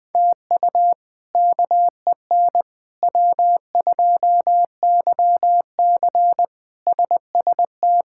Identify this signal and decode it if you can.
Morse code, heard as a keyed tone, about 21 wpm, 700 Hz. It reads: TU KEN W2YC SST
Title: K1USN SST QSO Practice - 20wpm